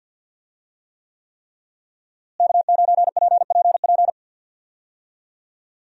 Morse code, heard as a keyed tone, about 46 wpm, 700 Hz. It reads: K9PPP
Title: USA Call Signs - 5 Characters - Repeated 3x - 50wpm